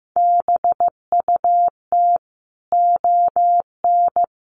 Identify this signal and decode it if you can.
Morse code, heard as a keyed tone, about 14 wpm, 700 Hz. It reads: BUT ON
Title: Sentences from Top 1400 Words 15wpm